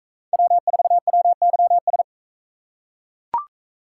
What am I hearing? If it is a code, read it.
Morse code, heard as a keyed tone, about 41 wpm, 700 Hz. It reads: W4WYS